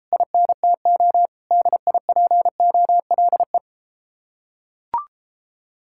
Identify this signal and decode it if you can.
Morse code, heard as a keyed tone, about 33 wpm, 700 Hz. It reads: INTO DIPOLE